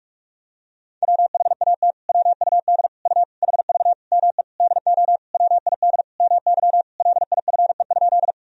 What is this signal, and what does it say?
Morse code, heard as a keyed tone, about 46 wpm, 700 Hz. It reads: WHAT WUD U HV ME DO WID MY LIFE?
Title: Sentences from Top 300 Words and Abbreviations 45wpm